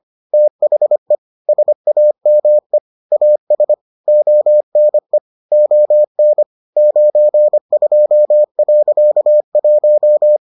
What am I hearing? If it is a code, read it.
Morse code, heard as a keyed tone, about 25 wpm, 600 Hz. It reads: THE SAME AS ONE ON 92.1